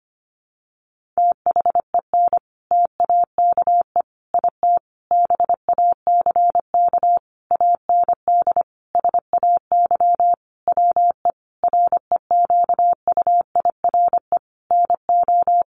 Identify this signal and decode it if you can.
Morse code, heard as a keyed tone, about 25 wpm, 700 Hz. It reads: THEN TAKE IT BACK AND SAY WE REQUIRE NO
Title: Sentences from Top 1500 Words 25wpm